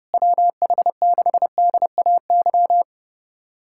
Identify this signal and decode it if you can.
Morse code, heard as a keyed tone, about 30 wpm, 700 Hz. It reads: WH6DAY
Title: USA Callsign DE Callsign - 6 Characters - 30wpm